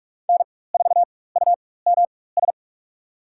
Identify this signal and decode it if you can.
Morse code, heard as a keyed tone, about 41 wpm, spaced as for 20 wpm, 700 Hz. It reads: N4UMS
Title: USA Call Signs - 5 Characters - 4x Character Spacing - 45wpm